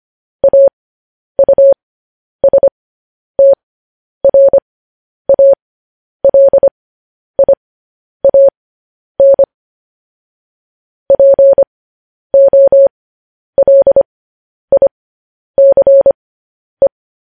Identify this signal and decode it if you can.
Morse code, heard as a keyed tone, about 25 wpm, spaced as for 10 wpm, 550 Hz. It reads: AUSTRALIAN POLICE
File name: podcast / News headlines-1x-25x10